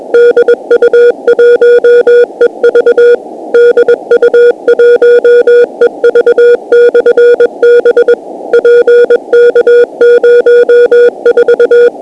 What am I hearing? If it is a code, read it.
Morse code, heard as a keyed tone, about 21 wpm, 480 Hz. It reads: DU1EV DU1EV/B PK04
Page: DU1EV/b Beacon 50.008 MHz Operated by Eddie DU1EV and located in PK04, QUEZON CITY, PHILIPPINES